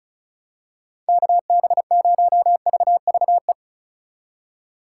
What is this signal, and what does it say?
Morse code, heard as a keyed tone, about 34 wpm, 700 Hz. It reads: KB0VVE
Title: USA Call Signs - 6 Characters - 4x Character Spacing - 35wpm